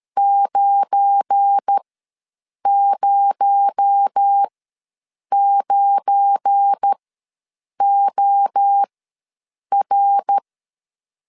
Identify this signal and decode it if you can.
Morse code, heard as a keyed tone, about 13 wpm, spaced as for 7 wpm, 800 Hz. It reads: 909OR
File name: LEZIONI / 7 GF9Z / 7 GF9 ws 12wpm 3344